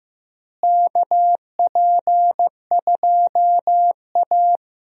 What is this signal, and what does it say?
Morse code, heard as a keyed tone, about 15 wpm, 700 Hz. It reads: KP2A
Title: DXCC-100 Call Signs - 4 Characters - Repeated 3x - 15wpm